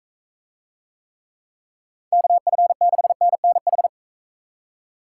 Morse code, heard as a keyed tone, about 44 wpm, 700 Hz. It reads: KF6NNH